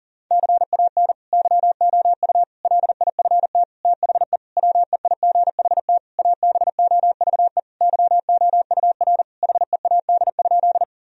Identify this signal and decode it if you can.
Morse code, heard as a keyed tone, about 40 wpm, 700 Hz. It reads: CAN YOU LIFT THE WEIGHT ABOVE YOUR HEAD?
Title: Sentences from Top 1000 Words - Review Part 3 - 40wpm